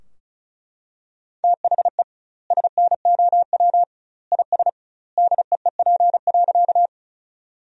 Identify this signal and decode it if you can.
Morse code, heard as a keyed tone, about 34 wpm, 700 Hz. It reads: THE SNOW IS DEEP.
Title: 2x Word-Spacing - Sentences from Top 500 Words - 35wpm